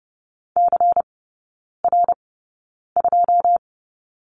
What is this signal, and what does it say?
Morse code, heard as a keyed tone, about 30 wpm, spaced as for 9 wpm, 700 Hz. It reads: CR2
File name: Lesson 16e - Letter c - Three Characters 30wpm